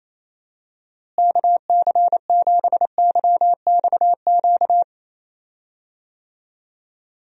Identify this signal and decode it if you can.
Morse code, heard as a keyed tone, about 28 wpm, 700 Hz. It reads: KC7YXQ
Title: USA Call Signs - 6 Characters - Repeated 3x - 28wpm